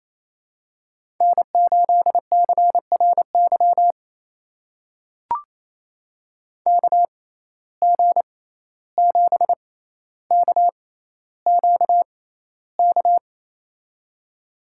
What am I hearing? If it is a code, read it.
Morse code, heard as a keyed tone, about 28 wpm, spaced as for 14 wpm, 700 Hz. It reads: N8CRY KG7KQK
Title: USA Call Signs - 6x Character Spacing - 28wpm